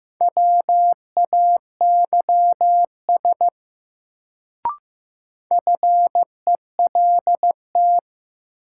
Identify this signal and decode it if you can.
Morse code, heard as a keyed tone, about 15 wpm, 700 Hz. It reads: WAYS FELT
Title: Sets of 5 Words Encore - New Word Drill - 15wpm